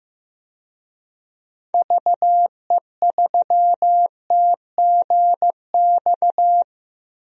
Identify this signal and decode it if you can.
Morse code, heard as a keyed tone, about 15 wpm, 700 Hz. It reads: VE3TGX
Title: DXCC-100 Call Signs - 6 Characters - 15wpm